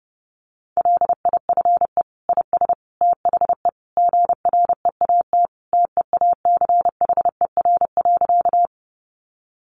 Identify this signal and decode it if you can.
Morse code, heard as a keyed tone, about 30 wpm, 700 Hz. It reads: LIFE IS THE GREAT TEACHER.
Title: Sentences from Top 900 Words 30wpm